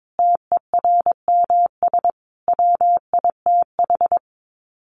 Morse code, heard as a keyed tone, about 22 wpm, 700 Hz. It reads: TERMS WITH
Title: Sets of 5 Words Encore 22wpm